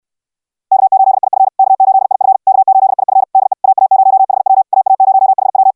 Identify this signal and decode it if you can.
Morse code, heard as a keyed tone, about 69 wpm, 780 Hz. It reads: K8IF K8IF K8IF DE AA0HW AA0HW